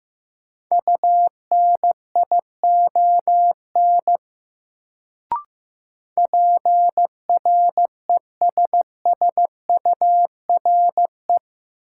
Morse code, heard as a keyed tone, about 15 wpm, 700 Hz. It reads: UNION PRESSURE
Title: Sentences from Top 1400 Words - New Word Drill Alt - 15wpm